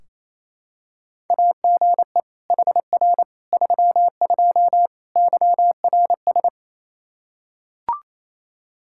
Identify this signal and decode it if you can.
Morse code, heard as a keyed tone, about 28 wpm, 700 Hz. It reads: AGE HR 32 YRS
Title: Practice QSO - Exchange 3 - More Info - 28wpm